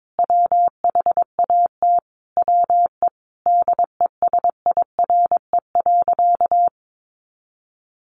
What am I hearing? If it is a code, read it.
Morse code, heard as a keyed tone, about 22 wpm, 700 Hz. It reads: WHAT WE DESIRE.
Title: Sentences from Top 1300 Words 22wpm